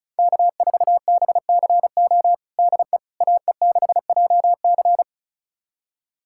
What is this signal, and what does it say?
Morse code, heard as a keyed tone, about 34 wpm, 700 Hz. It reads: K4BCO DE AE6JC
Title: USA Callsign DE Callsign - 5 Characters - 35wpm